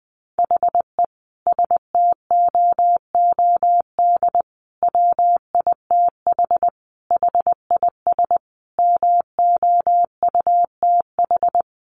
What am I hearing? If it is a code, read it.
Morse code, heard as a keyed tone, about 20 wpm, 700 Hz. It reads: HE STOOD WITH HIS MOUTH
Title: Sentences from Top 1500 Words 20wpm